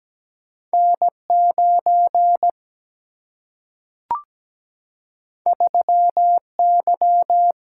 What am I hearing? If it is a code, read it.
Morse code, heard as a keyed tone, about 17 wpm, 700 Hz. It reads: N9 3Y
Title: Single-Letter Single-Number 17wpm